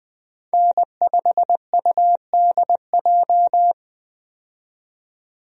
Morse code, heard as a keyed tone, about 20 wpm, 700 Hz. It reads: N5UDJ
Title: USA Call Signs - 5 Characters - 20wpm